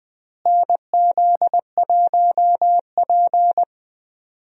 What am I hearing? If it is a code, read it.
Morse code, heard as a keyed tone, about 20 wpm, 700 Hz. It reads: NZ1P
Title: USA Call Signs - 4 Characters - Repeated 3x - 20wpm